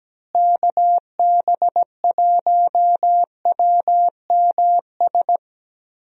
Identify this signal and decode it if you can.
Morse code, heard as a keyed tone, about 17 wpm, 700 Hz. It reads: KB1WMS
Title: USA Call Signs - 6 Characters - Repeated 2x - 17wpm